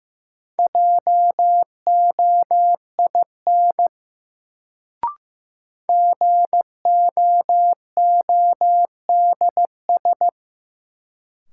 Morse code, heard as a keyed tone, about 15 wpm, 700 Hz. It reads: JOIN GOODS
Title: Sentences from Top 1000 Words - Review Part 6 - New Word Drill - 15wpm